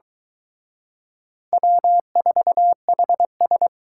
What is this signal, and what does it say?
Morse code, heard as a keyed tone, about 23 wpm, 700 Hz. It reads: W4HS